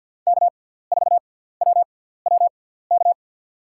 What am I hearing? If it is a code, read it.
Morse code, heard as a keyed tone, about 51 wpm, spaced as for 17 wpm, 700 Hz. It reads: K4WWK
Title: USA Call Signs - 5 Characters - 6x Character Spacing - 50wpm